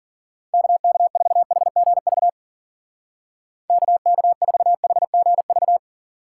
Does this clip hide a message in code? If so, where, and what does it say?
Yes: Morse code from 0.5 to 5.8 s, heard as a keyed tone, about 52 wpm, 700 Hz. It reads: KK4HGV KK4HGV